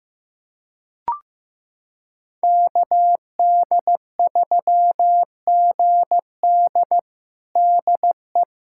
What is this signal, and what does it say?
Morse code, heard as a keyed tone, about 15 wpm, 700 Hz. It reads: KD3GD DE
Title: USA Callsign DE Callsign - 5 Characters - 15wpm